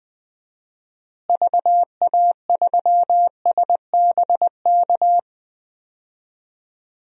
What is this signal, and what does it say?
Morse code, heard as a keyed tone, about 20 wpm, 700 Hz. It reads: VA3SBK
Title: DXCC-100 Call Signs - 6 Characters - Repeated 2x - 20wpm